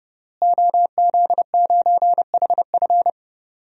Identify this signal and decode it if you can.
Morse code, heard as a keyed tone, about 32 wpm, 700 Hz. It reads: OZ9HF
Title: DXCC-100 Call Signs - 5 Characters - Repeated 3x - 30wpm